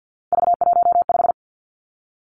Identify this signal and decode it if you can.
Morse code, heard as a keyed tone, about 50 wpm, 700 Hz. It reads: V15
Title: Three Challenging Letter-Number Combinations 50wpm